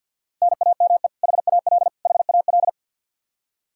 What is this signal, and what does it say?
Morse code, heard as a keyed tone, about 51 wpm, 700 Hz. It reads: NAME HAL HAL
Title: Practice QSO - Exchange 1 - Beginning - 50wpm